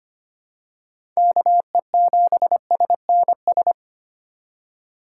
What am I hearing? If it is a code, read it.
Morse code, heard as a keyed tone, about 25 wpm, 700 Hz. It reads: KE7SNS
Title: USA Call Signs - 6 Characters - 4x Character Spacing - 25wpm